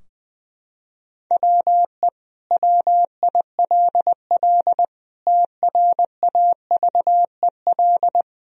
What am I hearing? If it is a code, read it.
Morse code, heard as a keyed tone, about 20 wpm, 700 Hz. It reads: WE WILL TRAVEL
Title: Sentences from Top 1000 Words - Review Part 2 - 20wpm